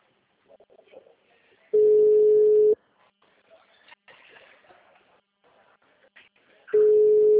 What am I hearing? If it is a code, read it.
Ringback tone (Europe, 425 Hz).
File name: out- -2005- - - .2538